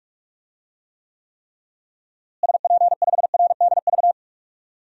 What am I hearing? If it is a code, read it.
Morse code, heard as a keyed tone, about 46 wpm, 700 Hz. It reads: SP5RDV
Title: DXCC-100 Call Signs - 6 Characters - Repeated 3x - 45wpm